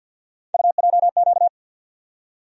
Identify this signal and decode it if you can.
Morse code, heard as a keyed tone, about 53 wpm, 700 Hz. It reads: UJQ
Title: USA Call Sign Postfixes - 3 Letters 50wpm